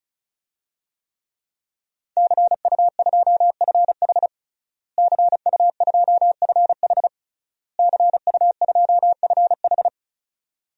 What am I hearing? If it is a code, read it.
Morse code, heard as a keyed tone, about 36 wpm, 700 Hz. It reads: CU2FH CU2FH CU2FH
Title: DXCC-100 Call Signs - 5 Characters - Repeated 3x - 35wpm